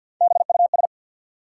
Morse code, heard as a keyed tone, about 53 wpm, 700 Hz. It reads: BUS
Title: Sentences from Top 1000 Words - Review Part 6 - New Word Drill - 50wpm